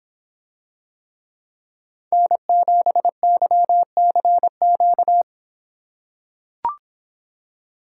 Morse code, heard as a keyed tone, about 24 wpm, 700 Hz. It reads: N7YCQ